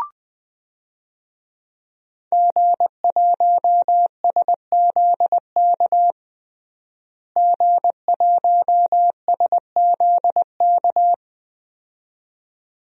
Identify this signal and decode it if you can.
Morse code, heard as a keyed tone, about 20 wpm, 700 Hz. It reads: G1SZK G1SZK
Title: DXCC-100 Call Signs - 5 Characters - Repeated 2x - 20wpm